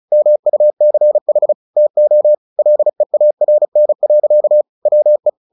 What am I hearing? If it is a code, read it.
Morse code, heard as a keyed tone, about 34 wpm, 600 Hz. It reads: MUCH TO LEARN. WE
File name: the old man and the sea 2 35wpm